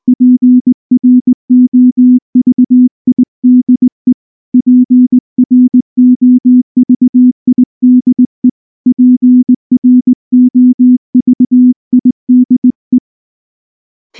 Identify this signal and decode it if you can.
Morse code, heard as a keyed tone, about 19 wpm, 250 Hz. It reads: PROVIDE PROVIDE PROVIDE
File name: morse-training-20WPM